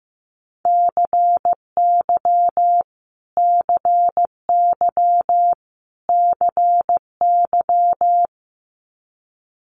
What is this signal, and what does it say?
Morse code, heard as a keyed tone, about 15 wpm, 700 Hz. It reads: CY CY CY
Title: Multi-letter Phonemes Repeated 3x 15wpm